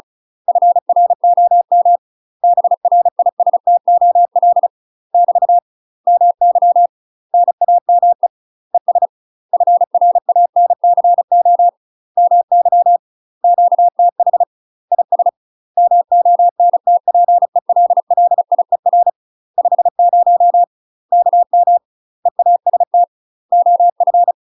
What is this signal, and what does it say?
Morse code, heard as a keyed tone, about 35 wpm, 700 Hz. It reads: FROM BRISTOL <BT> MY NAME ES FRANCO MY QTH IS MONTPELLIER 50 KM EAST OF
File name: UK MASTER QSOs TRAINING / 35WPM / UK MASTER QSOs 0032